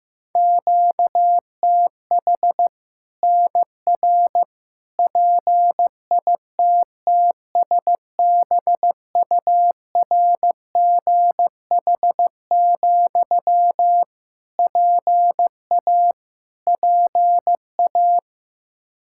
Morse code, heard as a keyed tone, about 15 wpm, 700 Hz. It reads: QTH NR PITTSBURGH, PA PA
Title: Practice QSO - Exchange 1 - Beginning - 15wpm